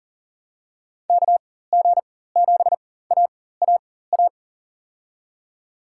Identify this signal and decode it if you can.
Morse code, heard as a keyed tone, about 40 wpm, spaced as for 19 wpm, 700 Hz. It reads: KG7AAA
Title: USA Call Signs - 6 Characters - 4x Character Spacing - 40wpm